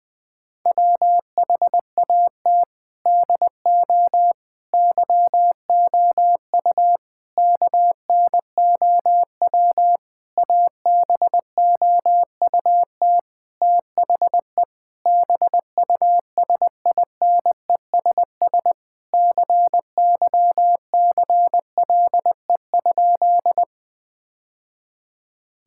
Morse code, heard as a keyed tone, about 20 wpm, 700 Hz. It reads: WHAT DO YOU KNOW ABOUT THE BUSINESS CYCLE?
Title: Sentences from Top 1000 Words - Review Part 6 - 20wpm